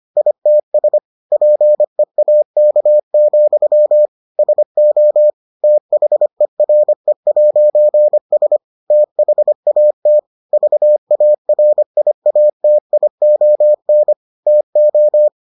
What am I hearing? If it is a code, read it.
Morse code, heard as a keyed tone, about 25 wpm, 600 Hz. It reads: ITS PEAK, SO THERE'S THAT VARIATION TO